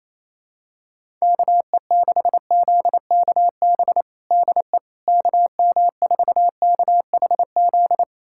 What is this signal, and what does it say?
Morse code, heard as a keyed tone, about 28 wpm, 700 Hz. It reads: KE6ZKB DE KM4KHZ
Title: USA Callsign DE Callsign 28wpm